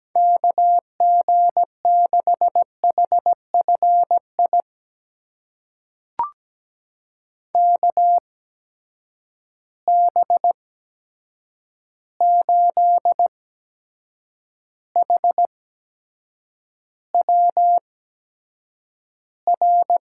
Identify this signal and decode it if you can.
Morse code, heard as a keyed tone, about 17 wpm, spaced as for 7 wpm, 700 Hz. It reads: KG6HFI KB8HWR
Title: USA Call Signs - 6 Characters - 8x Character Spacing - 17wpm